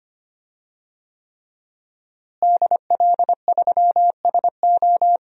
Morse code, heard as a keyed tone, about 25 wpm, 700 Hz. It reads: DL3SO
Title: DXCC-100 Call Signs - 5 Characters - Repeated 3x - 25wpm